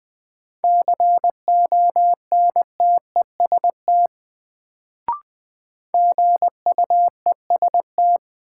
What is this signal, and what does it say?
Morse code, heard as a keyed tone, about 20 wpm, 700 Hz. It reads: CONTEST GUEST
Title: Sentences from Top 1600 Words - New Word Drill Alt - 20wpm